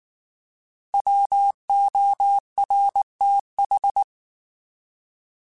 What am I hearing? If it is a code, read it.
Morse code, heard as a keyed tone, about 19 wpm, 800 Hz. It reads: WORTH
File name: Alphabetical / Alphabetical mp3s / Ws @18